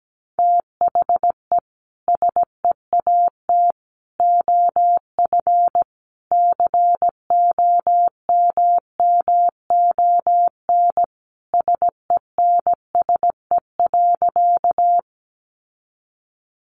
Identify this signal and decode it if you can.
Morse code, heard as a keyed tone, about 17 wpm, 700 Hz. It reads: THE SEAT OF COMMON SENSE.
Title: Sentences from Top 1500 Words 17wpm